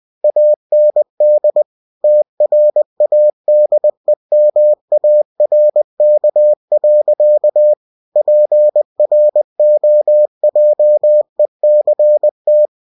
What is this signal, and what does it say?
Morse code, heard as a keyed tone, about 20 wpm, 600 Hz. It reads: AND TRADEMARK. PROJECT
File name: cw-books / winnie-the-pooh / winnie-20wpm-page-0112